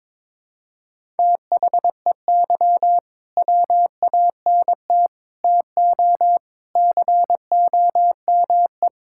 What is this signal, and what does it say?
Morse code, heard as a keyed tone, about 22 wpm, 700 Hz. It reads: THEY WANT TO COME
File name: practice / 22wpm / Sentences with 3-Word Phrases - 22wpm